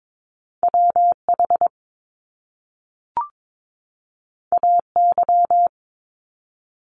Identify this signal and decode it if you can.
Morse code, heard as a keyed tone, about 22 wpm, 700 Hz. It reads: WH AY
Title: Top Two Letter Combinations in Words 22wpm